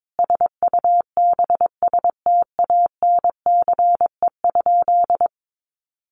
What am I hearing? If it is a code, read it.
Morse code, heard as a keyed tone, about 22 wpm, 700 Hz. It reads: SUBSTANCE?
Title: Sentences from Top 1500 Words 22wpm